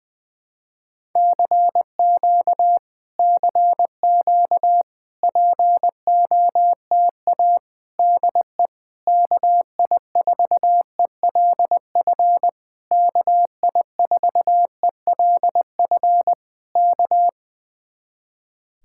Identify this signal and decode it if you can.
Morse code, heard as a keyed tone, about 20 wpm, 700 Hz. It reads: CQ CQ POTA DE KI4ELF KI4ELF K
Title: Speed-Racing Common QSO Phrases - 20wpm